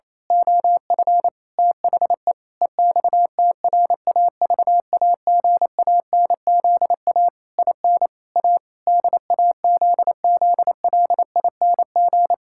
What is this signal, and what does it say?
Morse code, heard as a keyed tone, about 28 wpm, 700 Hz. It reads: OF THE EXTRAVAGANZA IN A DAZZLING
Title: Pangram Challenge v2 - 28wpm